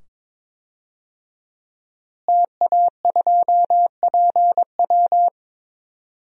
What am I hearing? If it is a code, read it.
Morse code, heard as a keyed tone, about 22 wpm, 700 Hz. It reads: TA2PW
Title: DXCC-100 Call Signs - 5 Characters - Repeated 2x - 22wpm